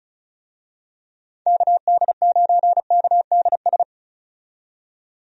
Morse code, heard as a keyed tone, about 36 wpm, 700 Hz. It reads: KD9KDS
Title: USA Call Signs - 6 Characters - 35wpm